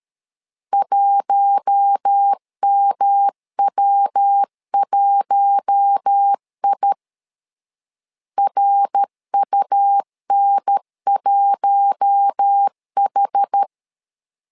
Morse code, heard as a keyed tone, about 13 wpm, 800 Hz. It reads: 1MW1I RUN1H
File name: LEZIONI / 4 Q1 12wpm 2028